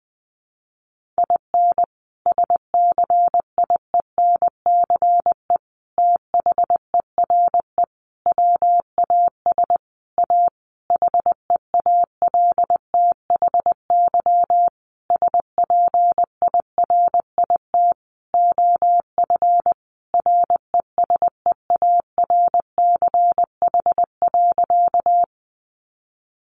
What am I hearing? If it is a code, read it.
Morse code, heard as a keyed tone, about 20 wpm, 700 Hz. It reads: IN SCIENCE THERE WAS A HEALTHY SPIRIT OF RESEARCH.
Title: Sentences from Top 1400 Words 20wpm